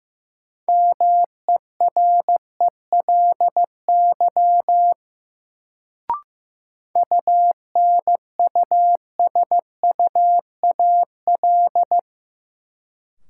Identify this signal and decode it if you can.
Morse code, heard as a keyed tone, about 15 wpm, 700 Hz. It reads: MERELY UNUSUAL
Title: Sentences from Top 1400 Words - New Word Drill - 15wpm